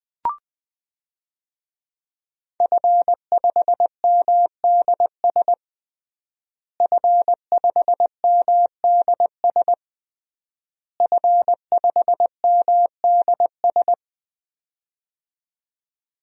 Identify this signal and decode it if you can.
Morse code, heard as a keyed tone, about 21 wpm, 700 Hz. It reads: F5MDS F5MDS F5MDS
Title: DXCC-100 Call Signs - 5 Characters - Repeated 3x - 20wpm